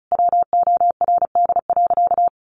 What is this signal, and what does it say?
Morse code, heard as a keyed tone, about 34 wpm, 700 Hz. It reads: WORD.
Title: A Sound Of Thunder 35wpm